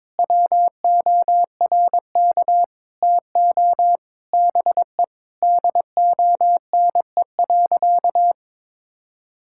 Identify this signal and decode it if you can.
Morse code, heard as a keyed tone, about 22 wpm, 700 Hz. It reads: WORK TO BE DONE.
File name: Sentences with 9-Word Phrases - 22wpm